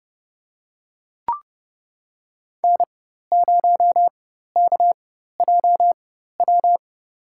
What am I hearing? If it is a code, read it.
Morse code, heard as a keyed tone, about 30 wpm, spaced as for 14 wpm, 700 Hz. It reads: N0KJW
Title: USA Call Signs - 4x Character Spacing - 30wpm